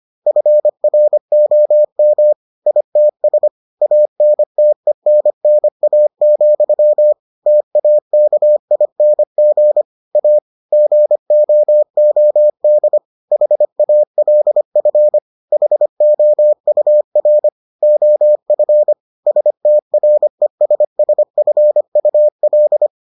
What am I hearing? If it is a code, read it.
Morse code, heard as a keyed tone, about 25 wpm, 600 Hz. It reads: FROM ITS ANTENNA, TAKING A GOOD HALF HOUR OF STRESSFUL